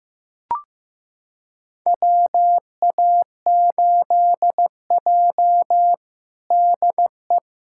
Morse code, heard as a keyed tone, about 15 wpm, 700 Hz. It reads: WA8J DE
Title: USA Callsign DE Callsign - 4 Characters - 15wpm